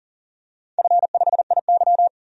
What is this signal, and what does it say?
Morse code, heard as a keyed tone, about 42 wpm, 700 Hz. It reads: F5IY
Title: DXCC-100 Call Signs - 4 Characters - 40wpm